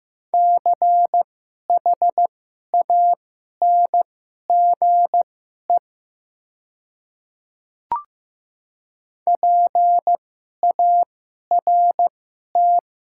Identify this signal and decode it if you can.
Morse code, heard as a keyed tone, about 15 wpm, spaced as for 11 wpm, 700 Hz. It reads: CHANGE PART
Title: 2x Character Spacing - Top 300 Words - 15wpm